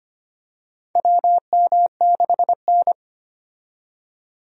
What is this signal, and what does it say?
Morse code, heard as a keyed tone, about 25 wpm, 700 Hz. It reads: WM6N
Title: USA Call Signs - 4 Characters - 25wpm